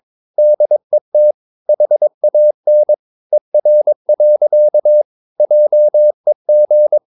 Morse code, heard as a keyed tone, about 23 wpm, 600 Hz. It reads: DET HAN ER. JEG
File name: Ferie QRS 22wpm